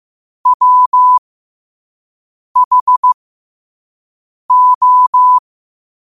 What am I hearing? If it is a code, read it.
Morse code, heard as a keyed tone, about 15 wpm, spaced as for 5 wpm, 1000 Hz. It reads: WHO
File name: qotdfiles / 5wpm